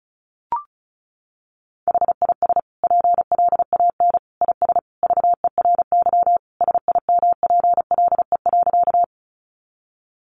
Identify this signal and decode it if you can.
Morse code, heard as a keyed tone, about 34 wpm, 700 Hz. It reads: HIS PLAN IS VERY SIMPLE.
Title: Sentences from Top 500 Words 35wpm